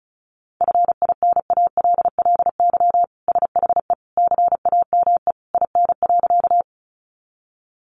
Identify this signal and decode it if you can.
Morse code, heard as a keyed tone, about 34 wpm, 700 Hz. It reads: FINALLY SHE CAME IN.
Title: Sentences from Top 1000 Words 35wpm